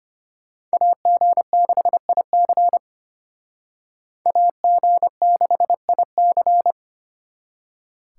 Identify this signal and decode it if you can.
Morse code, heard as a keyed tone, about 33 wpm, 700 Hz. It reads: AG6IC AG6IC